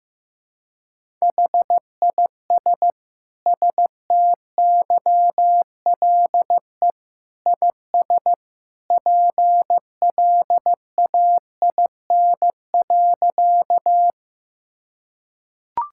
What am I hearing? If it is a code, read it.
Morse code, heard as a keyed tone, about 16 wpm, 700 Hz. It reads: HIS STYLE IS PLAIN.
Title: Sentences from Top 1000 Words - Review Part 5 - 15wpm